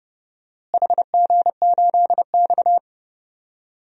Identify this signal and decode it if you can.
Morse code, heard as a keyed tone, about 29 wpm, 700 Hz. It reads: HG8X
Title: DXCC-100 Call Signs - 4 Characters - Repeated 2x - 30wpm